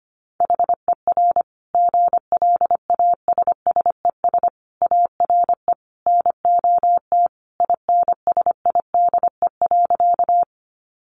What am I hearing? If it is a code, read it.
Morse code, heard as a keyed tone, about 25 wpm, 700 Hz. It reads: HER GLASSES ARE NOT INSIDE.
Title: Additional Easy Phrases 25wpm